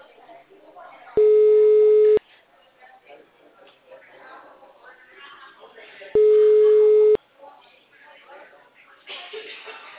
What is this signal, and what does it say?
Ringback tone (Europe, 425 Hz), with speech.
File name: out- -2030- - - .38